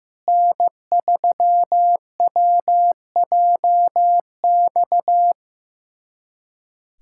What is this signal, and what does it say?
Morse code, heard as a keyed tone, about 15 wpm, 700 Hz. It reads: N3WJX
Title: USA Callsign DE Callsign 15wpm